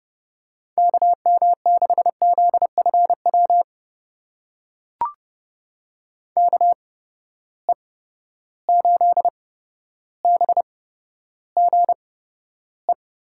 Morse code, heard as a keyed tone, about 30 wpm, spaced as for 13 wpm, 700 Hz. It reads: KM6ZFW KE8BGE
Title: USA Call Signs - 6 Characters - 8x Character Spacing - 30wpm